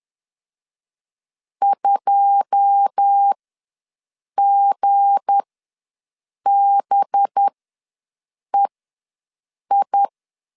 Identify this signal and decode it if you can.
Morse code, heard as a keyed tone, about 10 wpm, spaced as for 6 wpm, 800 Hz. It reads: 2GBEI
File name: LEZIONI / 10 SIGNS / 10B INT ws 10wpm 4134